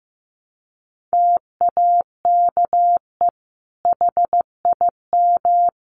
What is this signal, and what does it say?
Morse code, heard as a keyed tone, about 15 wpm, 700 Hz. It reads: TAKE HIM
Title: Sentences from Top 600 Words 15wpm